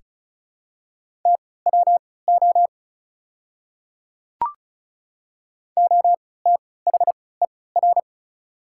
Morse code, heard as a keyed tone, about 36 wpm, spaced as for 20 wpm, 690 Hz. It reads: TWO OTHER
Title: 3x Character Spacing - Top 100 Words - 35wpm